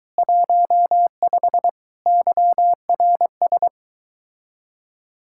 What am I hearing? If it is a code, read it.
Morse code, heard as a keyed tone, about 23 wpm, 700 Hz. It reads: 15 YRS